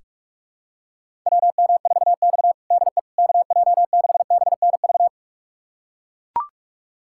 Morse code, heard as a keyed tone, about 45 wpm, 700 Hz. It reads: WM4X DE KJ6BNV
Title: USA Callsign DE Callsign 45wpm